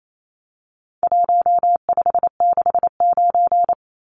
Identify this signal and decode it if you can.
Morse code, heard as a keyed tone, about 28 wpm, 700 Hz. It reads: 1569
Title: Numbers - Contest Style - 4-Digit 28wpm